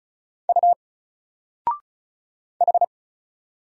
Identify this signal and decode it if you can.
Morse code, heard as a keyed tone, about 34 wpm, 700 Hz. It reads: U H
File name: Single-Letter-Number-35wpm